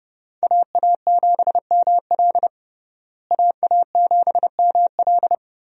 Morse code, heard as a keyed tone, about 30 wpm, 700 Hz. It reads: AA7ML AA7ML
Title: USA Call Signs - 5 Characters - Repeated 3x - 30wpm